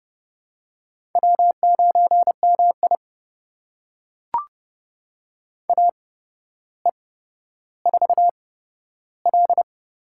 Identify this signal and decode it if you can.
Morse code, heard as a keyed tone, about 31 wpm, spaced as for 13 wpm, 700 Hz. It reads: W9MI AE4L
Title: USA Call Signs - 4 Characters - 8x Character Spacing - 30wpm